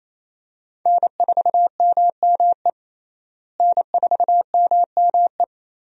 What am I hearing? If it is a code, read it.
Morse code, heard as a keyed tone, about 28 wpm, 700 Hz. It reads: N4MME N4MME
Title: USA Call Signs - 5 Characters - Repeated 3x - 28wpm